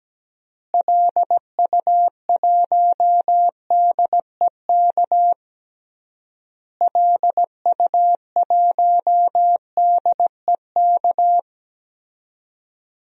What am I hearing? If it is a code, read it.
Morse code, heard as a keyed tone, about 17 wpm, 700 Hz. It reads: LU1DEK LU1DEK
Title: DXCC-100 Call Signs - 6 Characters - Repeated 3x - 17wpm